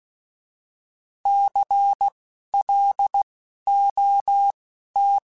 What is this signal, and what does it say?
Morse code, heard as a keyed tone, about 16 wpm, spaced as for 11 wpm, 800 Hz. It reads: CLOT
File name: Alphabetical / Alphabetical mp3s / Cs @10